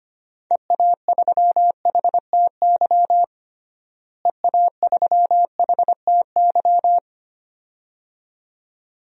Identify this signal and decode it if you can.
Morse code, heard as a keyed tone, about 25 wpm, 700 Hz. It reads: EA3HTY EA3HTY
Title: DXCC-100 Call Signs - 6 Characters - Repeated 2x - 25wpm